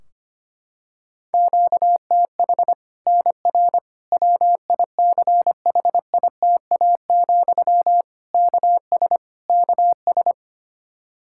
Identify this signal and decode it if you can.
Morse code, heard as a keyed tone, about 25 wpm, 700 Hz. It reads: QTH NR WICHITA, KS KS
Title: Practice QSO - Exchange 1 - Beginning - 25wpm